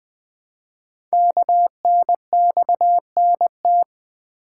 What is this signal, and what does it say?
Morse code, heard as a keyed tone, about 21 wpm, 700 Hz. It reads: KNXNT
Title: Code Groups - 5 Letters - 20wpm